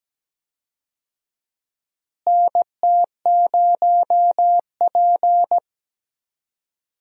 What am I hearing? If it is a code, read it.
Morse code, heard as a keyed tone, about 17 wpm, 700 Hz. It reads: NT0P
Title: USA Call Signs - 4 Characters - Repeated 3x - 17wpm